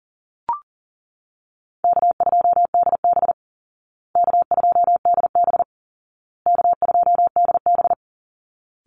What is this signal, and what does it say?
Morse code, heard as a keyed tone, about 40 wpm, 700 Hz. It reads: K2DB K2DB K2DB
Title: Easy Call Signs Repeated 3x - 40wpm